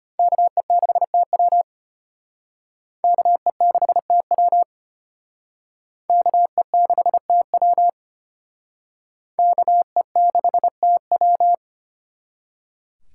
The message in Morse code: KE6TW KE6TW KE6TW KE6TW

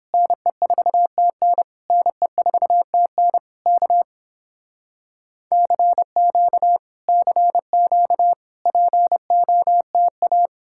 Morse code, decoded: NE4TN NE4TN K CQ CQ POTA

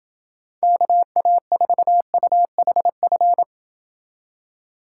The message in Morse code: KA4UHF

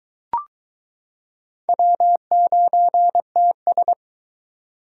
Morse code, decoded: W9TS

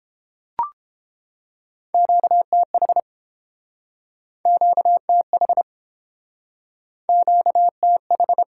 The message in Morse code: QTH QTH QTH

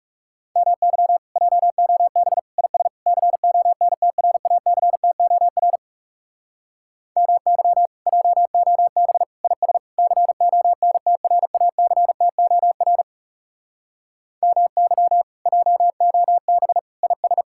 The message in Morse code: MY JOB IS CONTRACTOR MY JOB IS CONTRACTOR MY JOB IS

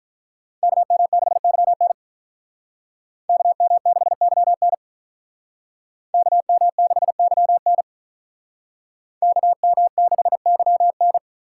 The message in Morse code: KM6YN KM6YN KM6YN KM6YN

